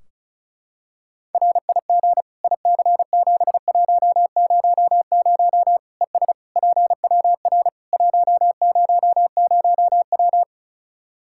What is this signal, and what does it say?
Morse code, heard as a keyed tone, about 34 wpm, 700 Hz. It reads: RIG IC7100 ES PWR 100W
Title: Practice QSO - Exchange 2 - Info - 35wpm